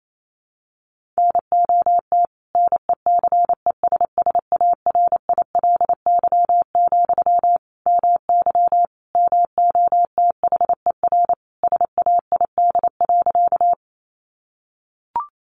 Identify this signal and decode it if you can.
Morse code, heard as a keyed tone, about 28 wpm, 700 Hz. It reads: NOT NECESSARILY, MY MOTHER SAID.
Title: Sentences from Top 1400 Words 28wpm